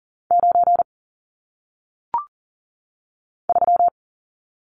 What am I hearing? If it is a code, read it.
Morse code, heard as a keyed tone, about 40 wpm, 700 Hz. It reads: 9 3
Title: Numbers - 1-Digit 40wpm